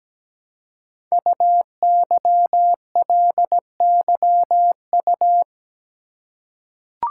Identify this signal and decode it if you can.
Morse code, heard as a keyed tone, about 17 wpm, 700 Hz. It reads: UYLYU
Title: Code Groups - 5 Letters - 17wpm